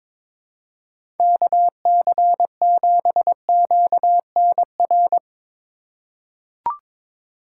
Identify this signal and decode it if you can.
Morse code, heard as a keyed tone, about 21 wpm, 700 Hz. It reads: KC7QNR
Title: USA Call Signs - 6 Characters - 3x Character Spacing - 22wpm